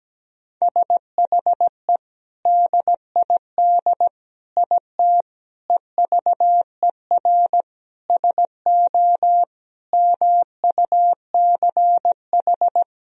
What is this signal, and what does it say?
Morse code, heard as a keyed tone, about 17 wpm, 700 Hz. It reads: SHE DID IT EVER SO MUCH
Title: Sentences from Top 400 Words and Abbreviations 17wpm